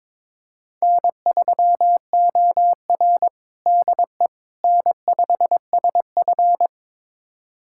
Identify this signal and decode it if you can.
Morse code, heard as a keyed tone, about 22 wpm, 700 Hz. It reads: N3OR DE N5SF
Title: USA Callsign DE Callsign - 4 Characters - 22wpm